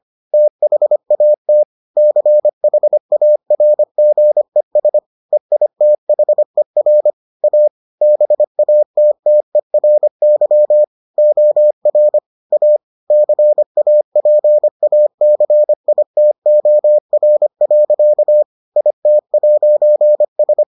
Morse code, heard as a keyed tone, about 25 wpm, 600 Hz. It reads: THAT CHARGES EITHER A BATTERY OR A CAPACITOR. IT'S